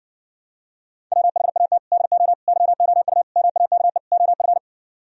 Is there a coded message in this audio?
Yes: Morse code from 1.1 to 4.6 s, heard as a keyed tone, about 60 wpm, 700 Hz. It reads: WHAT DO YOU MAKE OF